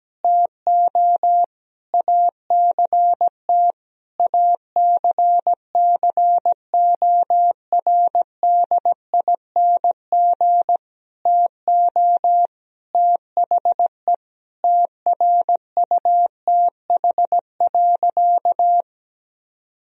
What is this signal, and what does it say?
Morse code, heard as a keyed tone, about 17 wpm, 700 Hz. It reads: TO ACT ACCORDING TO THE TRUTH.
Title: Sentences from Top 1000 Words - Review Part 4 - 17wpm